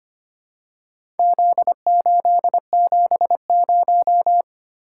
Morse code, heard as a keyed tone, about 25 wpm, 700 Hz. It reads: Z870
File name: practice / 25wpm / Four Letter-Number Combinations - 25wpm